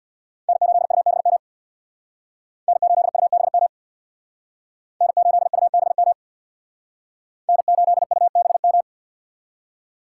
Morse code, heard as a keyed tone, about 60 wpm, 700 Hz. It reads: N8UBM N8UBM N8UBM N8UBM
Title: Speed Racing - USA 1x3 Call Signs 50wpm